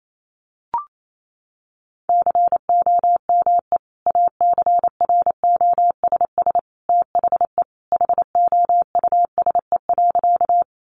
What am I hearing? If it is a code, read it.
Morse code, heard as a keyed tone, about 28 wpm, 700 Hz. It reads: COME ACROSS THE HOUSE.
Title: Sentences from Top 700 Words 28wpm